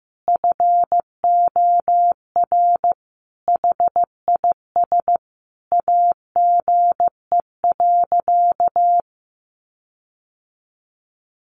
Morse code, heard as a keyed tone, about 15 wpm, 700 Hz. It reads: FOR HIS AGE.
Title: Sentences from Top 1300 Words 15wpm